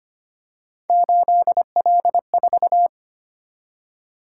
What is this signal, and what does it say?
Morse code, heard as a keyed tone, about 25 wpm, 700 Hz. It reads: 8L4
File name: Three Letter-Number Combinations - 25wpm